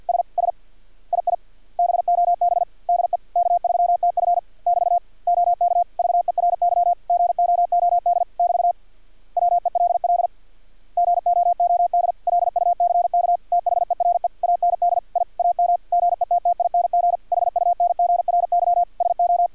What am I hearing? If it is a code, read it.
Morse code, heard as a keyed tone, about 50 wpm, 700 Hz. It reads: S S II BOB DE K3TF <BT> OK VERY GOOD <BT> WELL GOOD LUCK THERE AND I AM GETTING HUNGRY SO
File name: archived / CRSnet